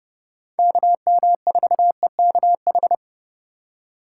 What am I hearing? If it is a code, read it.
Morse code, heard as a keyed tone, about 29 wpm, 700 Hz. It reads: KM4EKH